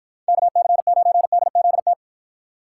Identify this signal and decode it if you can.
Morse code, heard as a keyed tone, about 50 wpm, 700 Hz. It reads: KC9DZT